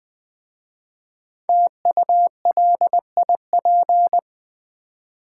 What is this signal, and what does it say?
Morse code, heard as a keyed tone, about 20 wpm, 700 Hz. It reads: TULIP
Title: Easy Words with Hints - 20wpm